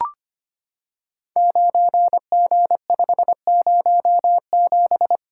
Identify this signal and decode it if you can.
Morse code, heard as a keyed tone, about 25 wpm, 700 Hz. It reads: 9G507
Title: Code Groups - 5 Letter:Numbers - 25wpm